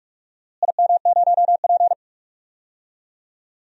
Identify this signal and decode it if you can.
Morse code, heard as a keyed tone, about 42 wpm, 690 Hz. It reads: IM0P
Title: DXCC-100 Call Signs - 4 Characters - Repeated 3x - 45wpm